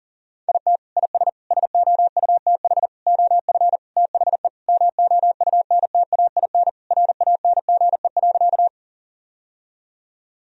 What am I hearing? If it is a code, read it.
Morse code, heard as a keyed tone, about 40 wpm, 700 Hz. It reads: IT IS SOUTH OF THE MOUNTAIN RANGE.
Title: Sentences from Top 1000 Words - Review Part 4 - 40wpm